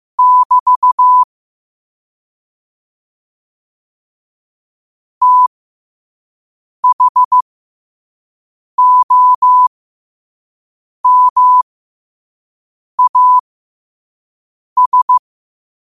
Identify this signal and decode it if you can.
Morse code, heard as a keyed tone, about 15 wpm, spaced as for 5 wpm, 1000 Hz. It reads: <BT> THOMAS
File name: qotdfiles / 5wpm